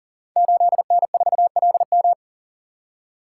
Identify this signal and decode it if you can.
Morse code, heard as a keyed tone, about 38 wpm, 700 Hz. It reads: 8N4LM
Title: Code Groups - 5 Letter:Numbers - 40wpm